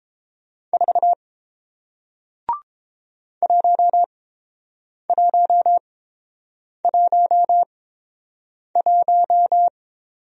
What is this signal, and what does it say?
Morse code, heard as a keyed tone, about 30 wpm, spaced as for 7 wpm, 700 Hz. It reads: 4 1111